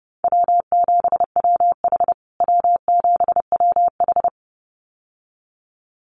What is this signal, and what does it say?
Morse code, heard as a keyed tone, about 30 wpm, 700 Hz. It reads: W7WH W7WH
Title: USA Call Signs Repeated 2x 30wpm